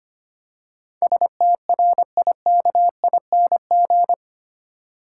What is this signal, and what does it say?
Morse code, heard as a keyed tone, about 25 wpm, 700 Hz. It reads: STRIKING
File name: 8 Letter Words - First 3 Letters Given - 25wpm